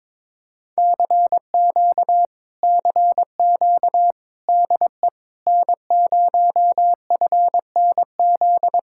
Morse code, heard as a keyed tone, about 22 wpm, 700 Hz. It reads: CQ CQ DE N0FNZ
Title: Speed-Racing Common QSO Phrases Encore - 22wpm